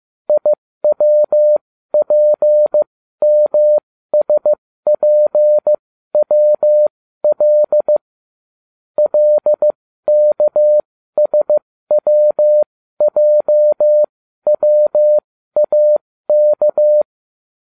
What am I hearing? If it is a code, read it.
Morse code, heard as a keyed tone, about 15 wpm, spaced as for 12 wpm, 600 Hz. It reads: IWPMSPWL LKSWJWAK